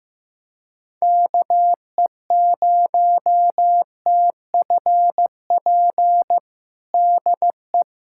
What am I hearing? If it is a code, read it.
Morse code, heard as a keyed tone, about 15 wpm, 700 Hz. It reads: KE0TFP DE
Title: USA Callsign DE Callsign - 6 Characters - 15wpm